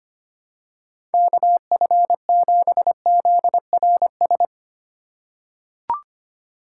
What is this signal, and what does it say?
Morse code, heard as a keyed tone, about 25 wpm, 700 Hz. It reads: KF7ZRS
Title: USA Call Signs - 6 Characters - 4x Character Spacing - 25wpm